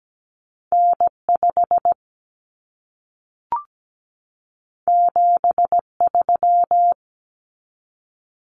Morse code, heard as a keyed tone, about 17 wpm, 700 Hz. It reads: N5 73
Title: Common Cut Numbers - 2-Digit 17wpm